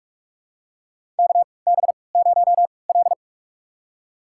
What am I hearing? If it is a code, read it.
Morse code, heard as a keyed tone, about 44 wpm, spaced as for 25 wpm, 690 Hz. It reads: KB0L
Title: USA Call Signs - 4 Characters - 3x Character Spacing - 45wpm